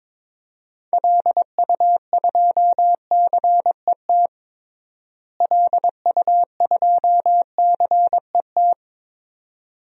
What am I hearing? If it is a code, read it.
Morse code, heard as a keyed tone, about 21 wpm, 700 Hz. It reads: LU2CET LU2CET
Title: DXCC-100 Call Signs - 6 Characters - Repeated 2x - 22wpm